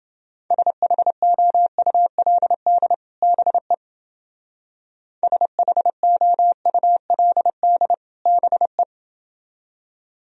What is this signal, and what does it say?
Morse code, heard as a keyed tone, about 29 wpm, 700 Hz. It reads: SHOULD BE SHOULD BE